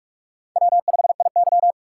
Morse code, heard as a keyed tone, about 46 wpm, 700 Hz. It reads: W5IY
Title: USA Call Signs - 4 Characters - 4x Character Spacing - 45wpm